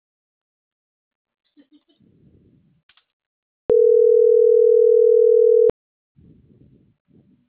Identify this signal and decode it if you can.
Ringback tone (North America), 440 + 480 Hz.